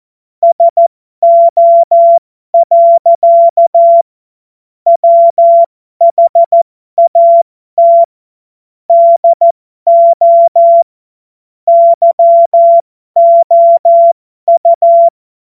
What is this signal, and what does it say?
Morse code, heard as a keyed tone, about 14 wpm, spaced as for 12 wpm, 680 Hz. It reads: SO. WHAT DO YOU
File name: Morsekurs / Witze